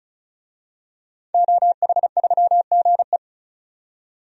Morse code, heard as a keyed tone, about 36 wpm, 700 Hz. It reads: OH3GE
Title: DXCC-100 Call Signs - 5 Characters - 35wpm